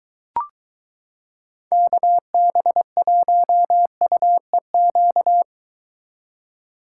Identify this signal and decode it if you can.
Morse code, heard as a keyed tone, about 22 wpm, 700 Hz. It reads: KB1UEQ